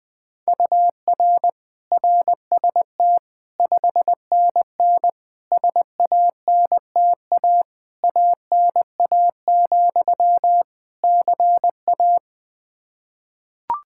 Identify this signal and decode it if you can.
Morse code, heard as a keyed tone, about 20 wpm, 700 Hz. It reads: UR RST 5NN SANTA ANA, CA
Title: Practice QSO - Exchange 1 - Beginning - 20wpm